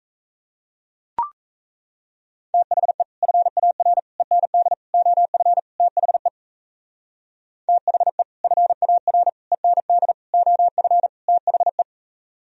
Morse code, heard as a keyed tone, about 34 wpm, 700 Hz. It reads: THE FAR END OF THE THE FAR END OF THE
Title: Speed-Racing - Sets of 5 Words 28wpm